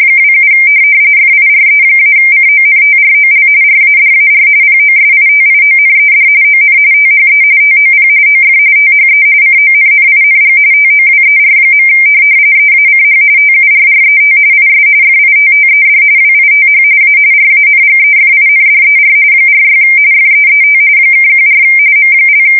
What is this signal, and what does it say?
RTTY, about 45.45 baud. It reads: 27 FIRE-CONTROL RADAR POSITIONED ON THE TOP OF THE CONNING TOWER. EACH TURRET IS FITTED WITH A RANGEFINDER 46 FEET (14 M)
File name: fsk170 / USS-Missouri